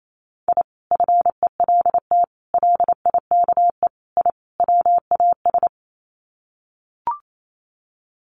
Morse code, heard as a keyed tone, about 28 wpm, 700 Hz. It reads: I FELT LIKE I WAS
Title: Sets of 5 Words Encore 28wpm